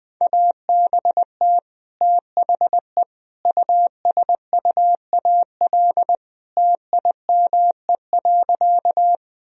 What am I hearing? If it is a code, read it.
Morse code, heard as a keyed tone, about 20 wpm, 700 Hz. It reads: ABT THE USUAL TIME.
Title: Sentences from Top 400 Words and Abbreviations 20wpm